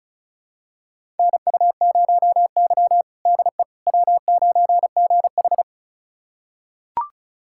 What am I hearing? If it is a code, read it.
Morse code, heard as a keyed tone, about 36 wpm, 700 Hz. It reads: NU0Y DE W9GH
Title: USA Callsign DE Callsign - 4 Characters - 35wpm